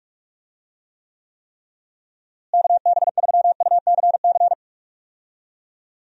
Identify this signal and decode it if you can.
Morse code, heard as a keyed tone, about 43 wpm, 700 Hz. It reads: KB3UCC
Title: USA Call Signs - 6 Characters - Repeated 3x - 45wpm